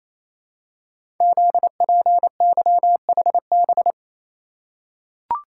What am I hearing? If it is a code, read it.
Morse code, heard as a keyed tone, about 28 wpm, 700 Hz. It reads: ZPYHB
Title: Code Groups - 5 Letter:Numbers - 28wpm